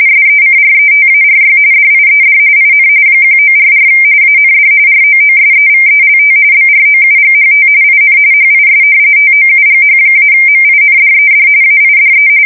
RTTY, about 45.45 baud. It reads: SECRETARY DIED IN 1944 AND ROOSEVELT BROUGHT IN HIS TOUGH-MINDED AIDE
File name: fsk170 / usnavy4